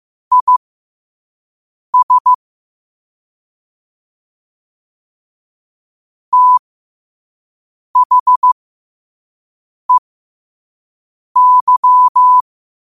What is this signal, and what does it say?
Morse code, heard as a keyed tone, about 15 wpm, spaced as for 5 wpm, 1000 Hz. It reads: IS THEY
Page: New quotes every day in morse code at 5 Words per minute.